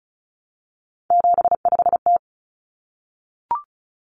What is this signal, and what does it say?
Morse code, heard as a keyed tone, about 34 wpm, 700 Hz. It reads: 75T
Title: Common Cut Numbers - 3-Digit 35wpm